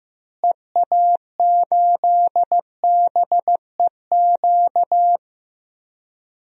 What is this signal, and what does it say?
Morse code, heard as a keyed tone, about 15 wpm, 700 Hz. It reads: EA8BEQ
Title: DXCC-100 Call Signs - 6 Characters - Repeated 2x - 15wpm